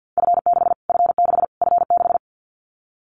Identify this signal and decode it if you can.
Morse code, heard as a keyed tone, about 50 wpm, 700 Hz. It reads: F6 F6 F6
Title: Callsign Prefixes for Top 25 DXCC - Repeated 3x - 50wpm